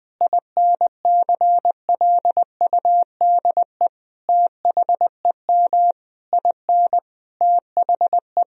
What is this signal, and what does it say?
Morse code, heard as a keyed tone, about 20 wpm, 700 Hz. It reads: INCLUDE THEM IN THE
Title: Sentences from Top 1000 Words - Review Part 3 - 20wpm